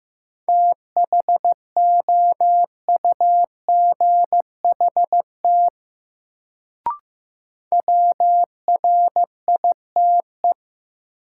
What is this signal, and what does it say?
Morse code, heard as a keyed tone, about 15 wpm, 700 Hz. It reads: THOUGHT WRITE
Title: Sentences from Top 300 Words - New Word Drill - 15wpm